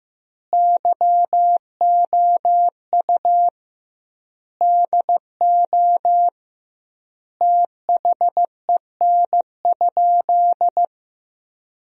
Morse code, heard as a keyed tone, about 15 wpm, 700 Hz. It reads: YOU DO THEN?
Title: 2x Word-Spacing - Sentences from Top 100 Words - 15wpm